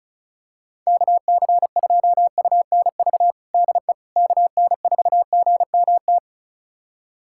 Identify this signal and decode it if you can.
Morse code, heard as a keyed tone, about 36 wpm, 700 Hz. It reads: KC2UNV DE KN4GMT
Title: USA Callsign DE Callsign - 6 Characters - 35wpm